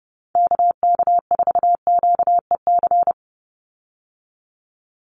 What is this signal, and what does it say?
Morse code, heard as a keyed tone, about 30 wpm, 700 Hz. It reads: KK4QEC
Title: USA Call Signs 30wpm